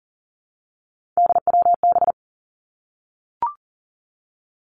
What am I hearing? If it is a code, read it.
Morse code, heard as a keyed tone, about 40 wpm, 700 Hz. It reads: DWB
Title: Three Challenging Letter-Number Combinations 40wpm